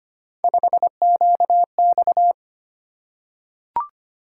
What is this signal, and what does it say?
Morse code, heard as a keyed tone, about 25 wpm, 700 Hz. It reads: 5QX
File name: Three Letter-Number Combinations - 25wpm